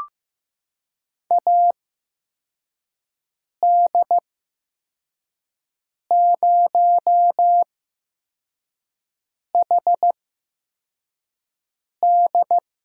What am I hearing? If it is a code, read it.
Morse code, heard as a keyed tone, about 15 wpm, spaced as for 4 wpm, 700 Hz. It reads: AD0HD
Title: USA Call Signs - 8x Character Spacing - 15wpm